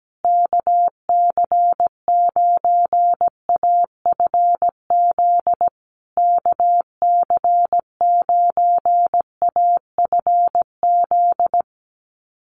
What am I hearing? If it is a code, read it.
Morse code, heard as a keyed tone, about 17 wpm, 700 Hz. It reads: KC9AFZ KC9AFZ
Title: USA Call Signs Repeated 3x 17wpm